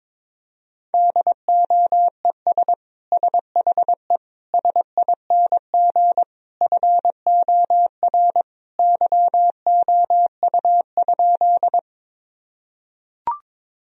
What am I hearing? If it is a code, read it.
Morse code, heard as a keyed tone, about 22 wpm, 700 Hz. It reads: DOES SHE SING FOR YOU?
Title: Sentences from Top 1000 Words - Review Part 2 - 22wpm